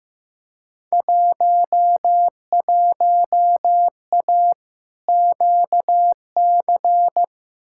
Morse code, heard as a keyed tone, about 15 wpm, 700 Hz. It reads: 11A QC
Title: Field Day Exchanges Encore - 15wpm